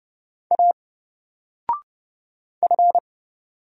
Morse code, heard as a keyed tone, about 30 wpm, 700 Hz. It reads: A F
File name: Single-Letter-Number-30wpm